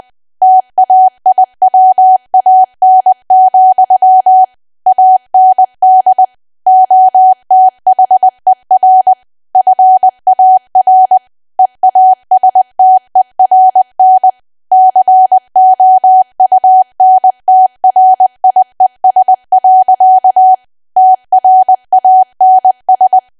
Morse code, heard as a keyed tone, about 20 wpm, 750 Hz. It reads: TAIWAN, AND OTHER FAR EASTERN COUNTRIES. TRANS
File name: cwa / W1AW-Level3 / 20WPM